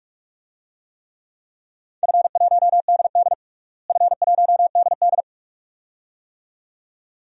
Morse code, heard as a keyed tone, about 42 wpm, 700 Hz. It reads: F1DD F1DD
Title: DXCC-100 Call Signs - 4 Characters - Repeated 2x - 45wpm